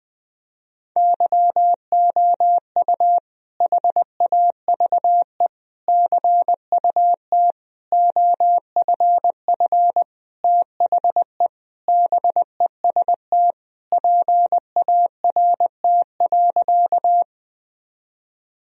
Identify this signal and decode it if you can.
Morse code, heard as a keyed tone, about 20 wpm, 700 Hz. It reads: YOU HAVE CUT OFF THE BEST PART.
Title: Sentences from Top 1000 Words - Review Part 1 - 20wpm